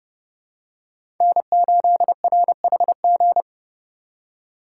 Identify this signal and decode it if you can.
Morse code, heard as a keyed tone, about 30 wpm, 700 Hz. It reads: N8RHG
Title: USA Call Signs - 5 Characters - 4x Character Spacing - 30wpm